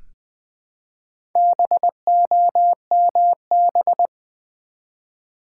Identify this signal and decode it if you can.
Morse code, heard as a keyed tone, about 20 wpm, 700 Hz. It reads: BOMB
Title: Build A Word - 1 Letter At A Time - 4-Letter Words - 20wpm